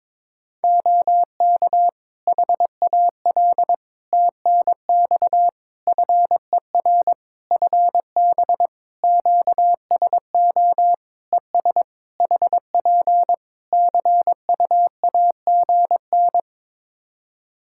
Morse code, heard as a keyed tone, about 22 wpm, 700 Hz. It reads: OK HAL TNX FER FB QSO ES HP CUAGN
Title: Practice QSO - Exchange 4 - Ending - 22wpm